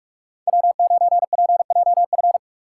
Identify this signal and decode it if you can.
Morse code, heard as a keyed tone, about 45 wpm, 700 Hz. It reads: W9PJF
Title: USA Call Signs - 5 Characters - 3x Character Spacing - 45wpm